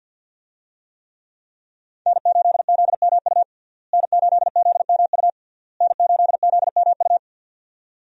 Morse code, heard as a keyed tone, about 50 wpm, 700 Hz. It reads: N8ZMU N8ZMU N8ZMU
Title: USA Call Signs - 5 Characters - Repeated 3x - 50wpm